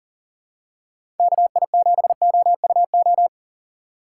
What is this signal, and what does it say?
Morse code, heard as a keyed tone, about 39 wpm, 700 Hz. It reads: KI7OUO
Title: USA Call Signs - 2x Character Spacing - 40wpm